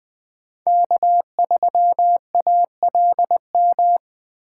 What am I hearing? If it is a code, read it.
Morse code, heard as a keyed tone, about 20 wpm, 700 Hz. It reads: K3ALM
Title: Speed Racing - USA 1x3 Call Signs 17wpm